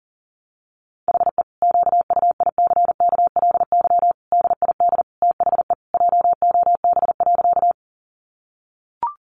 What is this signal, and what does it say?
Morse code, heard as a keyed tone, about 37 wpm, 700 Hz. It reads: HE QUICKLY DID THE JOB.
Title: Sentences from Top 700 Words 40wpm